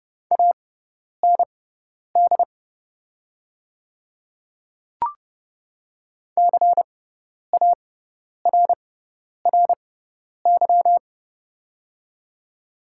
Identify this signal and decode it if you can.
Morse code, heard as a keyed tone, about 31 wpm, spaced as for 10 wpm, 700 Hz. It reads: AND CARRY
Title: 6x Character Spacing - Top 300 Words - 30wpm